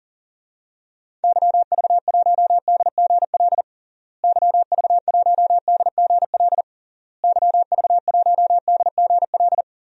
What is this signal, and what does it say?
Morse code, heard as a keyed tone, about 40 wpm, 700 Hz. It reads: YV1DGL YV1DGL YV1DGL
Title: DXCC-100 Call Signs - 6 Characters - Repeated 3x - 40wpm